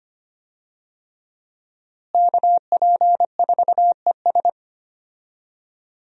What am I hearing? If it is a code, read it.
Morse code, heard as a keyed tone, about 25 wpm, 700 Hz. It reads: KP4ES
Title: USA Call Signs - 5 Characters - Repeated 3x - 25wpm